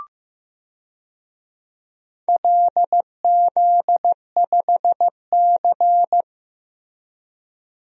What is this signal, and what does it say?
Morse code, heard as a keyed tone, about 16 wpm, 700 Hz. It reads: LZ5C
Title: DXCC-100 Call Signs - 4 Characters - Repeated 3x - 15wpm